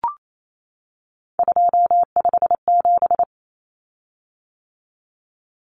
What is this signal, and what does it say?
Morse code, heard as a keyed tone, about 28 wpm, 700 Hz. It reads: 257
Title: Common Cut Numbers - 3-Digit 28wpm